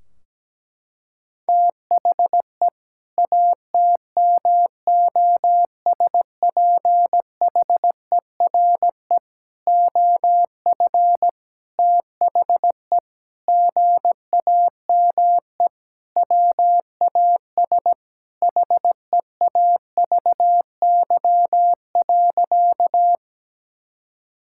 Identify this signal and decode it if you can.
Morse code, heard as a keyed tone, about 17 wpm, 700 Hz. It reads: THE ATMOSPHERE OF THE GAME WAS HEAVY.
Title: Sentences from Top 1600 Words 17wpm